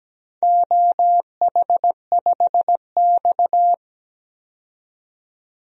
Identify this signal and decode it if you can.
Morse code, heard as a keyed tone, about 17 wpm, 700 Hz. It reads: OH5X
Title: DXCC-100 Call Signs - 4 Characters - 17wpm